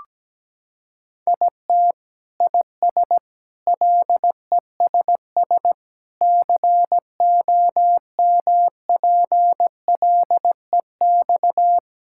Morse code, heard as a keyed tone, about 17 wpm, 700 Hz. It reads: IT IS LESS COMPLEX
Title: Sentences from Top 1000 Words - Review Part 6 - 17wpm